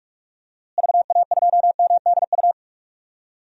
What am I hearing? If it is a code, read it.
Morse code, heard as a keyed tone, about 46 wpm, 700 Hz. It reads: VA2MDU
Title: DXCC-100 Call Signs - 6 Characters - Repeated 3x - 45wpm